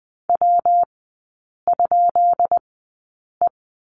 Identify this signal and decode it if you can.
Morse code, heard as a keyed tone, about 20 wpm, spaced as for 8 wpm, 700 Hz. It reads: W?E
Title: 138 Lesson 23e - Letter f - Three Characters 20wpm